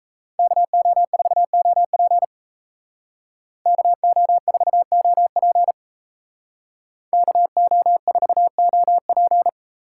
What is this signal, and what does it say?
Morse code, heard as a keyed tone, about 33 wpm, 700 Hz. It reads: KO4OP KO4OP KO4OP
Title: Speed Racing - USA 2x2 Call Signs 28wpm